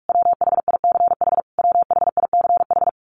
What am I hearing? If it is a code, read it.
Morse code, heard as a keyed tone, about 44 wpm, 700 Hz. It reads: WHICH WHICH
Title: Top 500 Words Repeated 45wpm